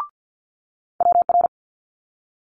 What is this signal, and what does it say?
Morse code, heard as a keyed tone, about 50 wpm, 700 Hz. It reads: WR
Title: Multi-letter Phonemes Quick Fire 50wpm